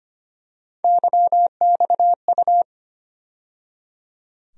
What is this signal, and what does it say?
Morse code, heard as a keyed tone, about 25 wpm, 700 Hz. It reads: YXU
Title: USA Call Sign Postfixes - 3 Letters 25wpm